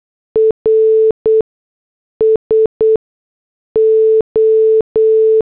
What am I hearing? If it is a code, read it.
Morse code, heard as a keyed tone, about 8 wpm, 440 Hz. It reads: R S O
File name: morse / headlines